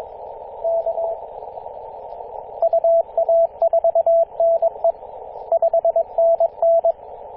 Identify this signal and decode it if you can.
Morse code, heard as a keyed tone, about 22 wpm, 670 Hz, with radio noise. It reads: K UA4NE 5NN
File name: posts / 2016-apr-24-1040z-v63gg-14-016mhz-cw